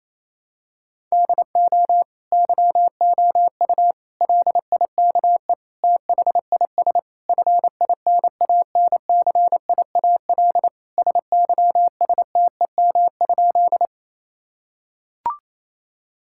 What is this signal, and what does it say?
Morse code, heard as a keyed tone, about 28 wpm, 700 Hz. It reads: DO YOU LIKE THIS FINANCIAL SYSTEM?
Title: Sentences from Top 1000 Words - Review Part 5 - 28wpm